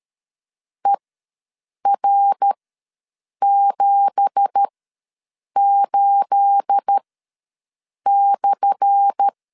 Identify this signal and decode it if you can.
Morse code, heard as a keyed tone, about 13 wpm, spaced as for 7 wpm, 800 Hz. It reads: ER78/
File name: LEZIONI / 10 SIGNS / 10A SLASH ws 12wpm 3443